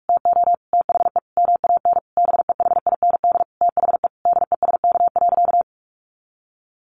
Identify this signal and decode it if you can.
Morse code, heard as a keyed tone, about 44 wpm, 700 Hz. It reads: TO THE MAN BEHIND THE DESK.
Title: A Sound Of Thunder 45wpm